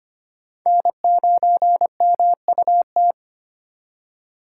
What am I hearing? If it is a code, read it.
Morse code, heard as a keyed tone, about 31 wpm, 700 Hz. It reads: N9MUT
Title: Speed Racing - USA 1x3 Call Signs 25wpm